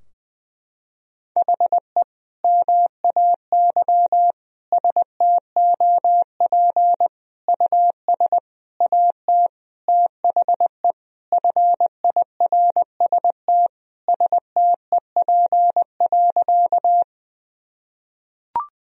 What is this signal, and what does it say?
Morse code, heard as a keyed tone, about 20 wpm, 700 Hz. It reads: HE MAY STOP US AT THE FIRST STEP.
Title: Sentences from Top 1000 Words - Review Part 2 - 20wpm